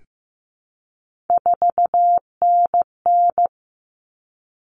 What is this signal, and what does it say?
Morse code, heard as a keyed tone, about 15 wpm, 700 Hz. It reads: 4NN
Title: Common Cut Numbers - 3-Digit 15wpm